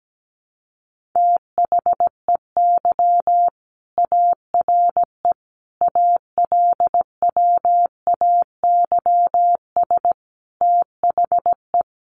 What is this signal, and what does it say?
Morse code, heard as a keyed tone, about 17 wpm, 700 Hz. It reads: THEY ARE ALWAYS THE
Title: Sentences from Top 300 Words 17wpm